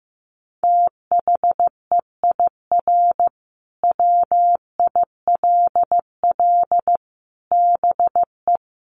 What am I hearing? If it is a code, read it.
Morse code, heard as a keyed tone, about 15 wpm, 700 Hz. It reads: THEIR WILL BE
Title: Sentences from Top 1500 Words 15wpm